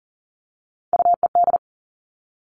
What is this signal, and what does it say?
Morse code, heard as a keyed tone, about 38 wpm, 690 Hz. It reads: UED
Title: Three Challenging Letter-Number Combinations 40wpm